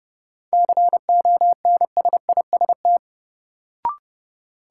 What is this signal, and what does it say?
Morse code, heard as a keyed tone, about 30 wpm, 700 Hz. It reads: CONSIST
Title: Sentences from Top 1400 Words - New Word Drill Alt - 30wpm